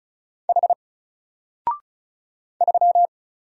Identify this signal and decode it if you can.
Morse code, heard as a keyed tone, about 33 wpm, 700 Hz. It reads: H 3
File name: Single-Letter-Number-35wpm